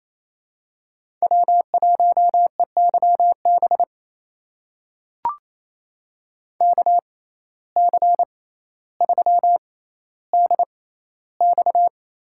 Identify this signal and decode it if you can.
Morse code, heard as a keyed tone, about 28 wpm, spaced as for 14 wpm, 700 Hz. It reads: W1EYB KC3DX
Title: USA Call Signs - 5 Characters - 6x Character Spacing - 28wpm